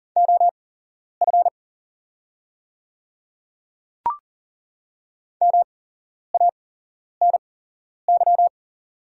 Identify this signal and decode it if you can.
Morse code, heard as a keyed tone, about 40 wpm, spaced as for 11 wpm, 690 Hz. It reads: OF MANY
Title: 8x Character Spacing - Top 100 Words - 40wpm